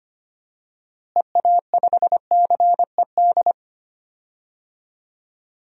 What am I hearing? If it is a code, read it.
Morse code, heard as a keyed tone, about 25 wpm, 700 Hz. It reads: EA5CED
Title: DXCC-100 Call Signs - 6 Characters - Repeated 3x - 25wpm